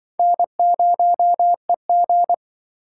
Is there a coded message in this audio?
Yes: Morse code from 0.2 to 2.3 s, heard as a keyed tone, about 26 wpm, 700 Hz. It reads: N0EG